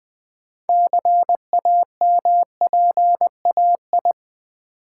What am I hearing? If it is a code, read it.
Morse code, heard as a keyed tone, about 20 wpm, 700 Hz. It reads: CAMPAI
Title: Build A Word - 2 Letters At A Time - 8-Letter Words - 20wpm